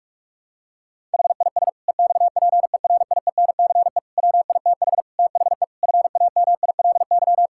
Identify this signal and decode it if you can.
Morse code, heard as a keyed tone, about 46 wpm, 700 Hz. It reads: HIS EXPERIENCE WITH THE FAMILY
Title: Sentences from Top 1000 Words - Review Part 6 - 45wpm